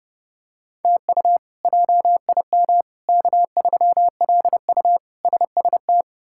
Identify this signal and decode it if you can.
Morse code, heard as a keyed tone, about 30 wpm, 700 Hz. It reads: TU JIM K3LU SST
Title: K1USN SST QSO Practice - 30wpm